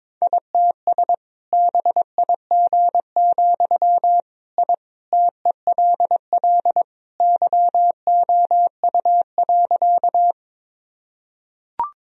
Morse code, heard as a keyed tone, about 22 wpm, 700 Hz. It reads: ITS BIG, I TELL YOU.
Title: Sentences from Top 1000 Words - Review Part 1 - 22wpm